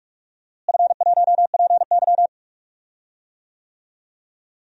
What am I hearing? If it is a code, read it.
Morse code, heard as a keyed tone, about 46 wpm, 700 Hz. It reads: F1PY
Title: DXCC-100 Call Signs - 4 Characters - Repeated 3x - 45wpm